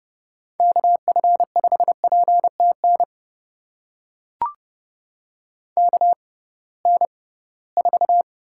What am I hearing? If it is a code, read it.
Morse code, heard as a keyed tone, about 30 wpm, spaced as for 19 wpm, 700 Hz. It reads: KF5PTN KN4
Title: USA Call Signs - 6 Characters - 6x Character Spacing - 30wpm